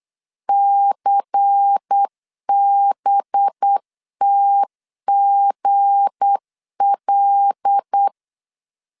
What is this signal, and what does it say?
Morse code, heard as a keyed tone, about 8 wpm, 800 Hz. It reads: CBTGL
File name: LEZIONI / 7 GF9Z / 7 GF9 8wpm 3545